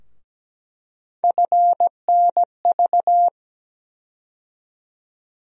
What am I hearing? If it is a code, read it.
Morse code, heard as a keyed tone, about 17 wpm, 700 Hz. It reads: FNV
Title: USA Call Sign Postfixes - 3 Letters 17wpm